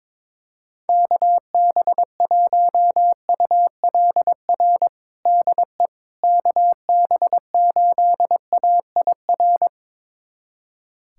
Morse code, heard as a keyed tone, about 22 wpm, 700 Hz. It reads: KB1ULR DE KB8AIR
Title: USA Callsign DE Callsign 22wpm